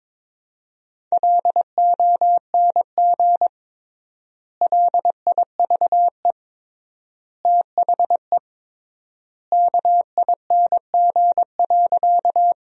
Morse code, heard as a keyed tone, about 22 wpm, 700 Hz. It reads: LONG LIVE THE KING.
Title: 3x Word-Spacing - Sentences from Top 400 Words - 22wpm